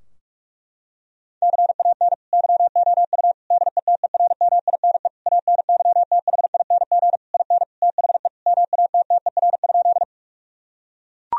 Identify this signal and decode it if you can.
Morse code, heard as a keyed tone, about 41 wpm, 700 Hz. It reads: CAN YOU DETERMINE ANYTHING IN THE MATTER?
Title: Sentences from Top 1000 Words - Review Part 4 - 45wpm